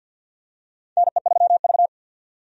Morse code, heard as a keyed tone, about 52 wpm, 700 Hz. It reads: NE3V
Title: USA Call Signs - 4 Characters - 50wpm